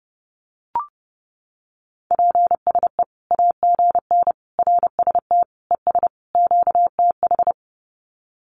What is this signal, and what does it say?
Morse code, heard as a keyed tone, about 30 wpm, 700 Hz. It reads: PSE AGN RST ES QTH
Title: Common QSO Phrases 30wpm